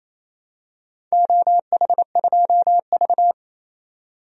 Morse code, heard as a keyed tone, about 28 wpm, 700 Hz. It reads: OH2V
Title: DXCC-100 Call Signs - 4 Characters - 28wpm